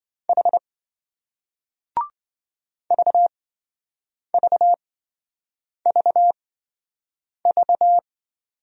Morse code, heard as a keyed tone, about 26 wpm, spaced as for 7 wpm, 700 Hz. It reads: H VVVV